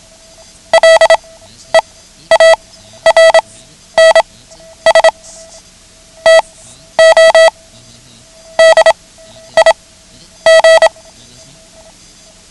Morse code, heard as a keyed tone, about 27 wpm, spaced as for 11 wpm, 650 Hz. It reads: LEARNS TO DIG
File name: ham radio / CodeQuick / Code Quick 09